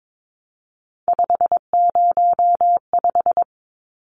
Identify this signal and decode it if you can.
Morse code, heard as a keyed tone, about 22 wpm, 700 Hz. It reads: 505
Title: Numbers - 3-Digit 22wpm